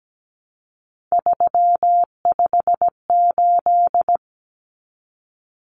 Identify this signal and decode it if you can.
Morse code, heard as a keyed tone, about 17 wpm, 700 Hz. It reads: 358
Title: Numbers - 3-Digit 17wpm